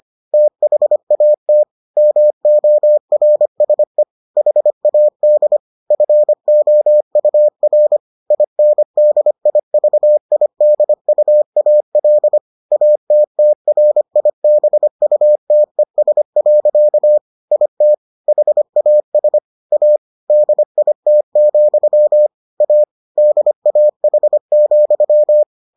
Morse code, heard as a keyed tone, about 25 wpm, 600 Hz. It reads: THAT MORSE HAD FOUR INDIVIDUAL ATTRIBUTES. IT HAS A DIT, A DAH,